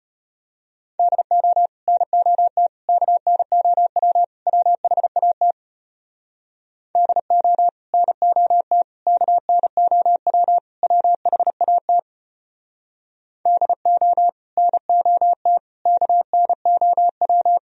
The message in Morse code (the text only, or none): DO NOT KNOW WHAT DO NOT KNOW WHAT DO NOT KNOW